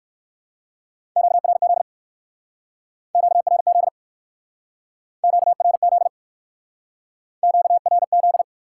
Morse code, heard as a keyed tone, about 61 wpm, 700 Hz. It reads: QRZ QRZ QRZ QRZ